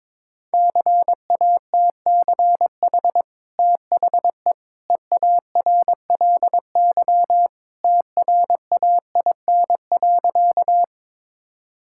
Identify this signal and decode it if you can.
Morse code, heard as a keyed tone, about 22 wpm, 690 Hz. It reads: CATCH THE EARLY TRAIN.
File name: practice / 22wpm / Easy Sentences with Hints - 22wpm